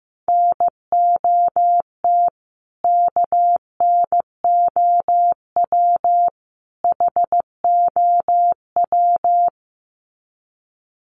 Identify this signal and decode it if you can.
Morse code, heard as a keyed tone, about 14 wpm, 700 Hz. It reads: NOT KNOW HOW
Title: Sets of 3 Words 15wpm